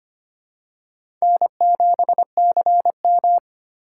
Morse code, heard as a keyed tone, about 25 wpm, 700 Hz. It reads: N7CM
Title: USA Call Signs - 4 Characters - 2x Character Spacing - 25wpm